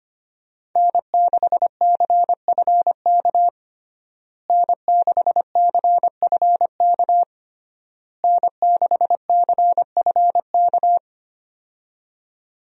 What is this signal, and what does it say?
Morse code, heard as a keyed tone, about 25 wpm, 700 Hz. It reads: N6CFK N6CFK N6CFK
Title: USA Call Signs - 5 Characters - Repeated 3x - 25wpm